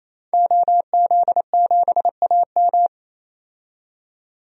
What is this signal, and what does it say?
Morse code, heard as a keyed tone, about 28 wpm, 700 Hz. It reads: OZ7AM
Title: DXCC-100 Call Signs - 5 Characters - 28wpm